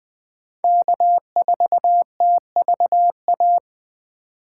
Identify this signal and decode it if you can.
Morse code, heard as a keyed tone, about 20 wpm, 700 Hz. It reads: K4TVA
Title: Speed Racing - USA 1x3 Call Signs 17wpm